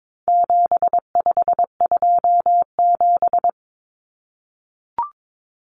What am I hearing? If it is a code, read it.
Morse code, heard as a keyed tone, about 22 wpm, 700 Hz. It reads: 7527
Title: Numbers - 4-Digit 22wpm